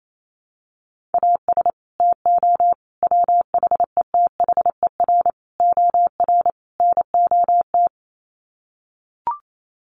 Morse code, heard as a keyed tone, about 28 wpm, 700 Hz. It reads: AS TO WHETHER OR NOT
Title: Sets of 5 Words Encore 28wpm